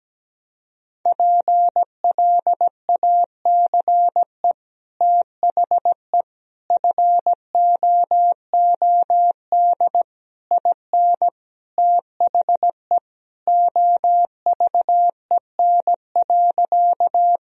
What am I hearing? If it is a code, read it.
Morse code, heard as a keyed tone, about 17 wpm, 700 Hz. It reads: PLACE THE FOOD IN THE OVEN.
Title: Sentences from Top 1000 Words - Review Part 4 - 17wpm